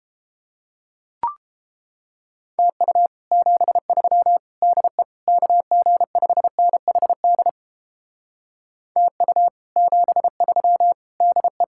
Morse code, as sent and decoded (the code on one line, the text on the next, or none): - ..- / --... ...-- / -.. / -.- --. ..... -. .... -.. / - ..- / --... ...-- / -.. .
TU 73 DE KG5NHD TU 73 DE